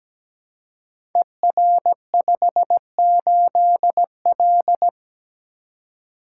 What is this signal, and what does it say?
Morse code, heard as a keyed tone, about 17 wpm, 700 Hz. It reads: ER58L
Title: Code Groups - 5 Letter:Numbers - 17wpm